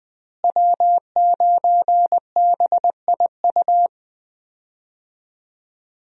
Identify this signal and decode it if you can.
Morse code, heard as a keyed tone, about 20 wpm, 700 Hz. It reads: W9BIU
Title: USA Call Signs - 5 Characters - Repeated 3x - 20wpm